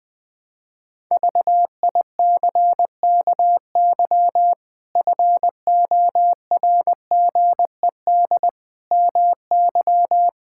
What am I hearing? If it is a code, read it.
Morse code, heard as a keyed tone, about 20 wpm, 700 Hz. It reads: VICKY FORGED MY
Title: Pangram Challenge v2 - 20wpm